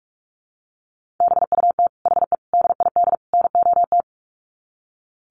Morse code, heard as a keyed tone, about 44 wpm, 700 Hz. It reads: BUT HE DID NOT
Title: Sets of 4 Words 45wpm